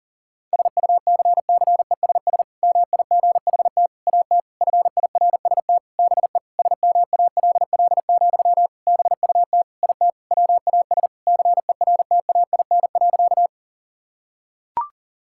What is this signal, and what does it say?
Morse code, heard as a keyed tone, about 40 wpm, 700 Hz. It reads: SUCCESS MIGHT AT FIRST BE SMALL, BUT IT WAS CERTAIN.
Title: Sentences from Top 1000 Words - Review Part 6 - 40wpm